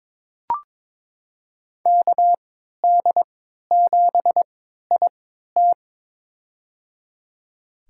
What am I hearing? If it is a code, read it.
Morse code, heard as a keyed tone, about 22 wpm, spaced as for 13 wpm, 700 Hz. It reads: KD7IT
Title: USA Call Signs - 3x Character Spacing - 22wpm